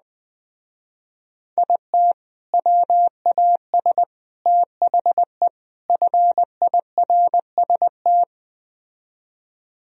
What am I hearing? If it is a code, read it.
Morse code, heard as a keyed tone, about 20 wpm, 700 Hz. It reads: IT WAS THE FIRST
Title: Speed-Racing - Sets of 4 Words 17wpm